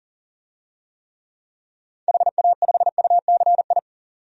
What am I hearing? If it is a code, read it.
Morse code, heard as a keyed tone, about 40 wpm, 700 Hz. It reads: HA5UCI
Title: DXCC-100 Call Signs - 6 Characters - 40wpm